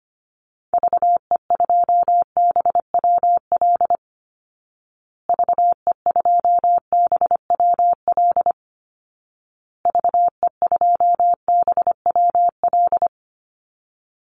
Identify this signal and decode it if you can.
Morse code, heard as a keyed tone, about 25 wpm, 700 Hz. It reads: VE2BWL VE2BWL VE2BWL
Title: Call Signs Repeated 2x - 25wpm